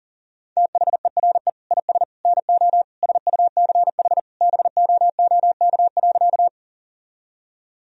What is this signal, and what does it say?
Morse code, heard as a keyed tone, about 40 wpm, 700 Hz. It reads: THERE IS NO SUCH BOOK.
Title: Sentences from Top 1000 Words - Review Part 1 - 40wpm